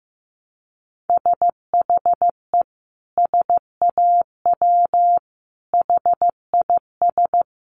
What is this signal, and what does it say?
Morse code, heard as a keyed tone, about 15 wpm, 700 Hz. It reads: SHE SAW HIS
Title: Sentences from Top 1300 Words 15wpm